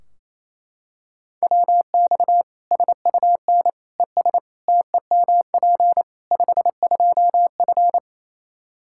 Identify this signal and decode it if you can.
Morse code, heard as a keyed tone, about 28 wpm, 700 Hz. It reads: WX SUN ES TEMP 52F
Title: Practice QSO - Exchange 2 - Info - 28wpm